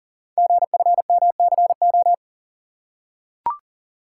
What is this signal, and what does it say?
Morse code, heard as a keyed tone, about 40 wpm, 690 Hz. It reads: GFMCO
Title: Code Groups - 5 Letters - 40wpm